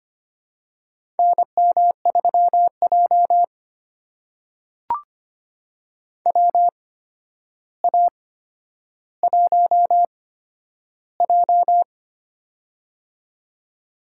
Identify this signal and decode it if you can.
Morse code, heard as a keyed tone, about 25 wpm, spaced as for 11 wpm, 700 Hz. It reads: NM3J WA1J
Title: USA Call Signs - 4 Characters - 8x Character Spacing - 25wpm